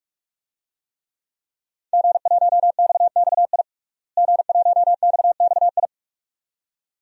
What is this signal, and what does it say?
Morse code, heard as a keyed tone, about 44 wpm, 700 Hz. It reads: G1XXI G1XXI
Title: DXCC-100 Call Signs - 5 Characters - Repeated 2x - 45wpm